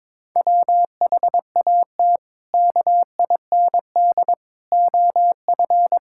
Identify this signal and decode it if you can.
Morse code, heard as a keyed tone, about 22 wpm, 700 Hz. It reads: WHAT KIND OF
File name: practice / 22wpm / Sentences with 3-Word Phrases - 22wpm